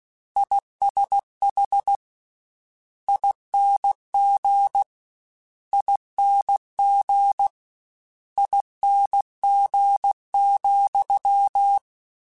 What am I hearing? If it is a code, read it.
Morse code, heard as a keyed tone, about 16 wpm, 800 Hz. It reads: ISH ING ING ING,
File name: Combinations / Combinations-mp3s / Combinations@15